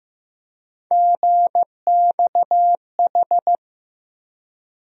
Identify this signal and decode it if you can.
Morse code, heard as a keyed tone, about 15 wpm, 700 Hz. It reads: GXH
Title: USA Call Sign Postfixes - 3 Letters 15wpm